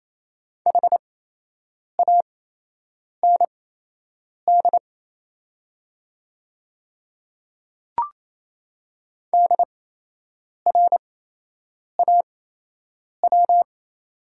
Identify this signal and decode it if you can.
Morse code, heard as a keyed tone, about 28 wpm, spaced as for 8 wpm, 700 Hz. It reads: HAND DRAW
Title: 8x Character Spacing - Top 300 Words - 28wpm